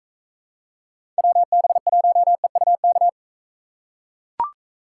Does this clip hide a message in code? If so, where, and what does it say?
Yes: Morse code from 1.2 to 3.1 s, heard as a keyed tone, about 43 wpm, 700 Hz. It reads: WB1EUK